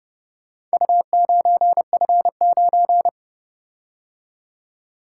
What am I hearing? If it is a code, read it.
Morse code, heard as a keyed tone, about 29 wpm, 700 Hz. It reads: U9F9